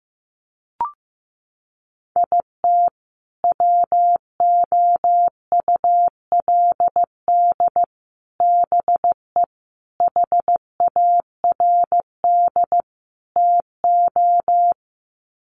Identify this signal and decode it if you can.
Morse code, heard as a keyed tone, about 15 wpm, 700 Hz. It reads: IT WOULD BE HARD TO
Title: Sets of 5 Words Encore2 15wpm